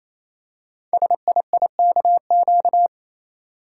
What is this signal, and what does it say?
Morse code, heard as a keyed tone, about 28 wpm, 700 Hz. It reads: SIIKQ
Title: Code Groups - 5 Letters - 28wpm